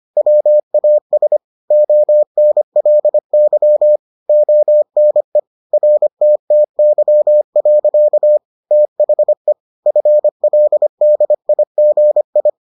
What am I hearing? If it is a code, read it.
Morse code, heard as a keyed tone, about 25 wpm, 600 Hz. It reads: WAS ONLY ONE RTTY. THE FLDIGI